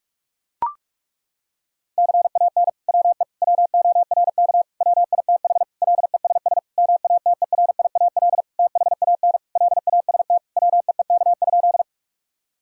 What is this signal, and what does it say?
Morse code, heard as a keyed tone, about 41 wpm, 700 Hz. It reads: CAN WE WORK WITH LESS MATERIAL THAN LAST WEEK?
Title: Sentences from Top 1000 Words - Review Part 5 - 45wpm